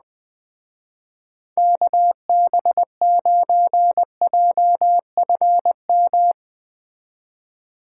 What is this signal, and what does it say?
Morse code, heard as a keyed tone, about 19 wpm, 700 Hz. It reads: KB9JFM